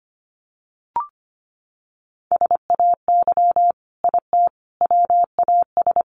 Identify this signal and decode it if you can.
Morse code, heard as a keyed tone, about 25 wpm, 700 Hz. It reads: SAY IT WAS
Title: Sentences from Top 800 Words 25wpm